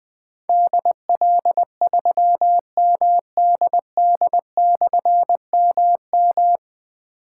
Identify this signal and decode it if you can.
Morse code, heard as a keyed tone, about 20 wpm, 700 Hz. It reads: DL3MDD/MM
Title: Difficult Call Signs 20wpm